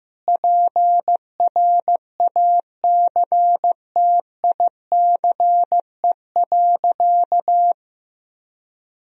Morse code, heard as a keyed tone, about 16 wpm, 700 Hz. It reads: PRACTICE.
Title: Sentences from Top 1000 Words - Review Part 4 - 15wpm